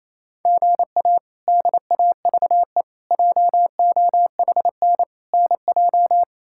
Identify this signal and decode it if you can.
Morse code, heard as a keyed tone, about 28 wpm, 700 Hz. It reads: GA DAVE JOHN NJ
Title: K1USN SST QSO Practice - 28wpm